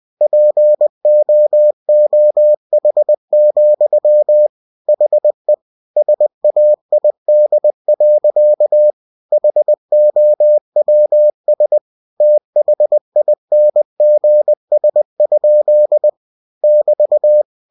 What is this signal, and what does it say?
Morse code, heard as a keyed tone, about 20 wpm, 600 Hz. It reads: POOH, HE SAID. HOWS THINGS? <BT>
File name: cw-books / winnie-the-pooh / winnie-20wpm-page-0029